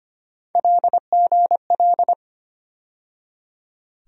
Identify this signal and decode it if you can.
Morse code, heard as a keyed tone, about 25 wpm, 700 Hz. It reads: LGL
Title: USA Call Sign Postfixes - 3 Letters 25wpm